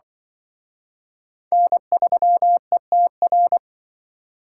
Morse code, heard as a keyed tone, about 24 wpm, 700 Hz. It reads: N3ETR
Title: Speed Racing - USA 1x3 Call Signs 20wpm